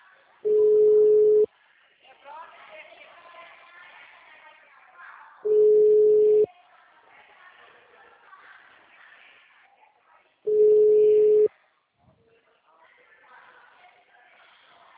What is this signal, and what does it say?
Ringback tone (Europe, 425 Hz), with speech.